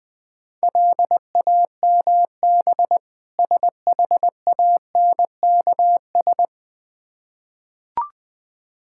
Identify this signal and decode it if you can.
Morse code, heard as a keyed tone, about 20 wpm, 700 Hz. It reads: LAMB SHANKS
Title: Difficult Words with Hints - 20wpm